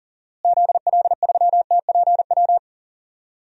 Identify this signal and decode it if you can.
Morse code, heard as a keyed tone, about 40 wpm, 700 Hz. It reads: ZL3TPW
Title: DXCC-100 Call Signs - 6 Characters - Repeated 3x - 40wpm